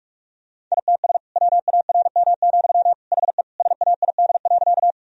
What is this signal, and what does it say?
Morse code, heard as a keyed tone, about 46 wpm, 700 Hz. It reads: ITS WARM, HE SAID.
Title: Sentences from Top 1000 Words - Review Part 2 - 45wpm